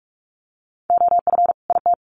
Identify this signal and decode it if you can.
Morse code, heard as a keyed tone, about 44 wpm, 700 Hz. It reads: OF IT
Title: Sets of 2 Words 45wpm